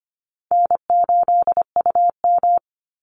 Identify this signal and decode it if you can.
Morse code, heard as a keyed tone, about 25 wpm, 700 Hz. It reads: N8UM
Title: Easy Call Signs Repeated 2x - 25wpm